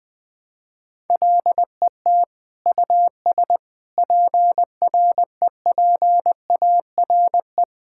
Morse code, heard as a keyed tone, about 20 wpm, 700 Hz. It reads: LET US PREPARE
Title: Sentences from Top 1000 Words - Review Part 6 - 20wpm